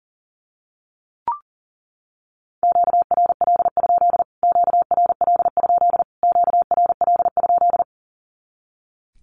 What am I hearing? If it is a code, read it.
Morse code, heard as a keyed tone, about 40 wpm, 700 Hz. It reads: QRL? QRL? QRL?
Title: Q-Codes Repeated 3x 40wpm